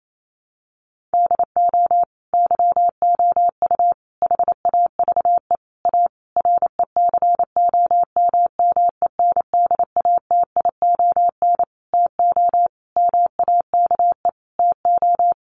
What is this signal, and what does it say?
Morse code, heard as a keyed tone, about 28 wpm, 700 Hz. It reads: DO YOU HAVE A RECOMMENDATION TO MAKE TO
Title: Sentences from Top 1400 Words 28wpm